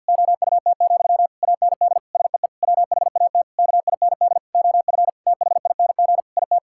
Morse code, heard as a keyed tone, about 50 wpm, 700 Hz. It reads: OUT, AND SEE WHAT KIND OF THING IT
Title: Sentences from Top 1000 Words - Review Part 6 - 50wpm